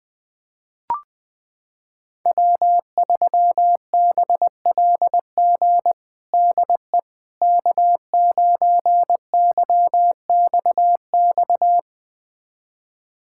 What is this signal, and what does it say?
Morse code, heard as a keyed tone, about 20 wpm, 700 Hz. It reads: W3BLG DE K9YXX
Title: USA Callsign DE Callsign - 5 Characters - 20wpm